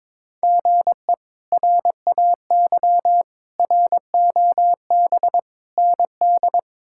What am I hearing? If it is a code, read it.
Morse code, heard as a keyed tone, about 22 wpm, 700 Hz. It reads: GE RAY ROB ND
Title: K1USN SST QSO Practice - 22wpm